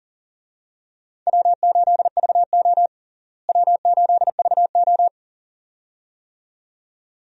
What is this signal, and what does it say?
Morse code, heard as a keyed tone, about 40 wpm, 700 Hz. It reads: W8VO W8VO
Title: USA Call Signs - 4 Characters - Repeated 2x - 40wpm